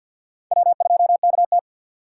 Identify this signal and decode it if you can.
Morse code, heard as a keyed tone, about 51 wpm, 700 Hz. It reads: W2KT
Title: USA Call Signs - 4 Characters - 2x Character Spacing - 50wpm